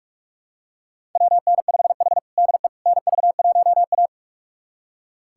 Morse code, heard as a keyed tone, about 46 wpm, 700 Hz. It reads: WN5H DE NV1A
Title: USA Callsign DE Callsign - 4 Characters - 45wpm